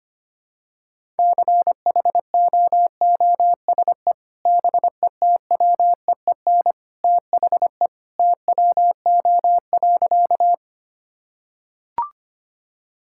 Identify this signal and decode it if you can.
Morse code, heard as a keyed tone, about 25 wpm, 700 Hz. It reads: CHOOSE BETWEEN THE TWO.
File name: practice / 25wpm / Sentences with 2-Word Phrases - 25wpm